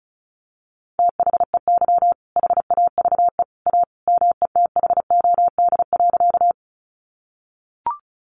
Morse code, heard as a keyed tone, about 34 wpm, 700 Hz. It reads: THEY HAVE A METHOD.
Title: Sentences from Top 600 Words 35wpm